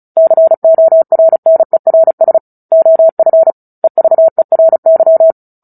Morse code, heard as a keyed tone, about 34 wpm, 650 Hz. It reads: CORNERS OF EVERY
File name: Book Two / 09-Chapter 9 - 35 WPM